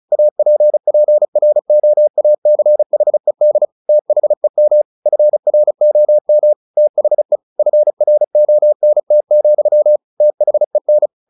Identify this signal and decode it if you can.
Morse code, heard as a keyed tone, about 34 wpm, 600 Hz. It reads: APPROACHED THEM FROM THE FRONT, THEN
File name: the old man and the sea / the old man and the sea 1 35wpm